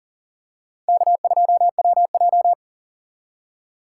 Morse code, heard as a keyed tone, about 40 wpm, 700 Hz. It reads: K2WJ
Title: USA Call Signs - 4 Characters - 2x Character Spacing - 40wpm